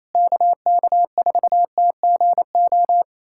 Morse code, heard as a keyed tone, about 28 wpm, 700 Hz. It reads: KK4TGO
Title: SOTA QSO Practice 28wpm